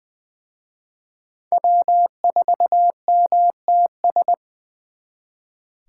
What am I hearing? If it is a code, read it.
Morse code, heard as a keyed tone, about 24 wpm, 700 Hz. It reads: W4MTS
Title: Speed Racing - USA 1x3 Call Signs 20wpm